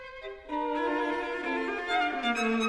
There is no code.